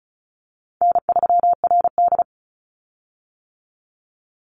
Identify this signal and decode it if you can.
Morse code, heard as a keyed tone, about 34 wpm, 700 Hz. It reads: N3RD
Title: Call Signs - 35wpm